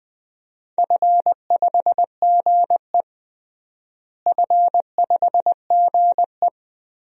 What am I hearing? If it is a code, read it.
Morse code, heard as a keyed tone, about 20 wpm, 700 Hz. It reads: F5GE F5GE
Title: DXCC-100 Call Signs - 4 Characters - Repeated 2x - 20wpm